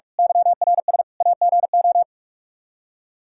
Morse code, heard as a keyed tone, about 46 wpm, 700 Hz. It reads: YRS AGO
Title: Practice QSO - Exchange 3 - More Info - 45wpm